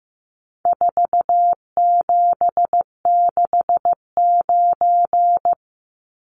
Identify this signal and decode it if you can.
Morse code, heard as a keyed tone, about 15 wpm, 700 Hz. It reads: 4769
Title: Numbers - 4-Digit 15wpm